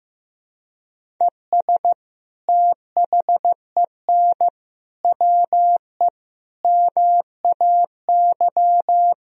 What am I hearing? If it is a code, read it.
Morse code, heard as a keyed tone, about 15 wpm, 700 Hz. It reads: ES THEN WE MAY
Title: Sentences from Top 300 Words and Abbreviations 15wpm